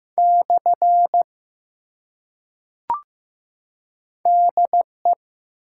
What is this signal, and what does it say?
Morse code, heard as a keyed tone, about 15 wpm, 700 Hz. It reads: / DE
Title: Top 180 QSO Elements 15wpm